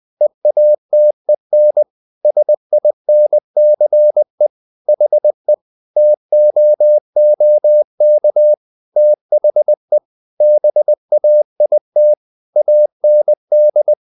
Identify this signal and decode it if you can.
Morse code, heard as a keyed tone, about 20 wpm, 600 Hz. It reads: EATEN SINCE HE TOOK THE BAIT AND
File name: the old man and the sea part 2 20wpm